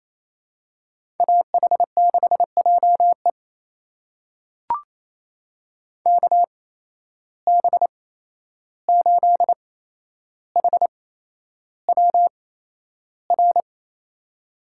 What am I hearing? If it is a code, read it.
Morse code, heard as a keyed tone, about 28 wpm, spaced as for 11 wpm, 700 Hz. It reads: AH6JE KB8HWR
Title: USA Call Signs - 8x Character Spacing - 28wpm